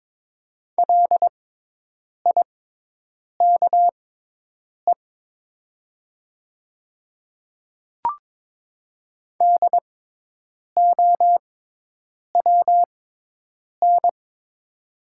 Morse code, heard as a keyed tone, about 22 wpm, spaced as for 8 wpm, 700 Hz. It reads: LIKE DOWN
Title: 6x Character Spacing - Top 200 Words - 22wpm